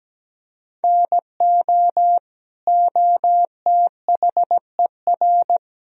Morse code, heard as a keyed tone, about 17 wpm, 700 Hz. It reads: NO OTHER
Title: Sentences from Top 1000 Words - Review Part 4 - 17wpm